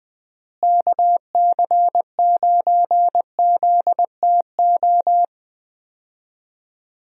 Morse code, heard as a keyed tone, about 21 wpm, 700 Hz. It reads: KC9ZTO
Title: USA Call Signs - 6 Characters - Repeated 3x - 20wpm